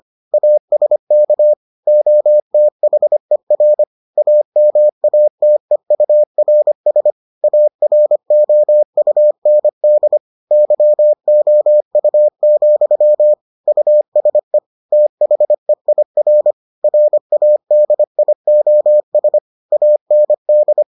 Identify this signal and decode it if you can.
Morse code, heard as a keyed tone, about 25 wpm, 600 Hz. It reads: ASK OTHER AMATEURS AROUND YOU, USE THEIR RADIOS AND